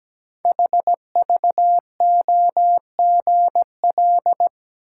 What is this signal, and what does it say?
Morse code, heard as a keyed tone, about 17 wpm, 700 Hz. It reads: HVOGL
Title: Code Groups - 5 Letter:Numbers - 17wpm